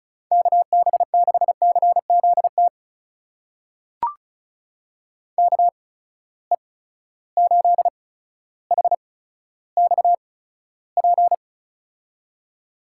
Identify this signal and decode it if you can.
Morse code, heard as a keyed tone, about 36 wpm, spaced as for 15 wpm, 700 Hz. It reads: KB6CZT KE8HXP
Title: USA Call Signs - 6 Characters - 8x Character Spacing - 35wpm